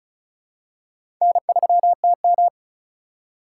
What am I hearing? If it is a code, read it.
Morse code, heard as a keyed tone, about 36 wpm, 700 Hz. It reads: N3TM
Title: USA Call Signs - 4 Characters - 2x Character Spacing - 35wpm